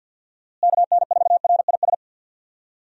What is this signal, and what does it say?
Morse code, heard as a keyed tone, about 45 wpm, 700 Hz. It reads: KN4RIS
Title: USA Call Signs - 6 Characters - Repeated 3x - 50wpm